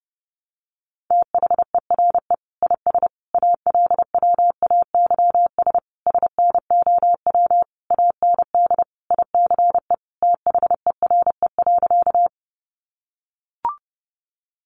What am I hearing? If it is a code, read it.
Morse code, heard as a keyed tone, about 30 wpm, 700 Hz. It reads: THERE IS ALWAYS SNOW AND ICE THERE.
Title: Sentences from Top 1000 Words 30wpm